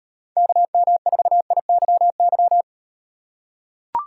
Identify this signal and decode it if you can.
Morse code, heard as a keyed tone, about 38 wpm, 700 Hz. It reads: KM4IYY